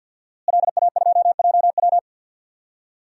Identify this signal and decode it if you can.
Morse code, heard as a keyed tone, about 52 wpm, 700 Hz. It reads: LA2JW
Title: DXCC-100 Call Signs - 5 Characters - Repeated 2x - 50wpm